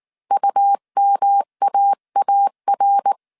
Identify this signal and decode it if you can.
Morse code, heard as a keyed tone, about 19 wpm, 800 Hz. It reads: UMAAR